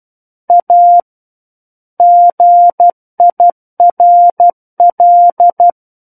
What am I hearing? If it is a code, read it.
Morse code, heard as a keyed tone, about 12 wpm, 700 Hz. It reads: A GIRL
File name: Book One / 03-Chapter 3 - 12 WPM